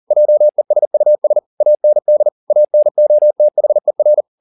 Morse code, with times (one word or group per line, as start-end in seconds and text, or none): 0.1-1.4 JESUS
1.6-2.3 AND
2.5-4.2 ANOTHER